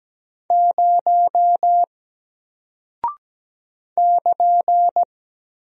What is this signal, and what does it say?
Morse code, heard as a keyed tone, about 17 wpm, 700 Hz. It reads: 0 <KN>
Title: Single Character (Letter:Number:Punctuation:Prosign) Alt 17wpm